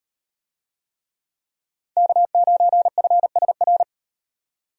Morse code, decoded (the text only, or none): K9FSR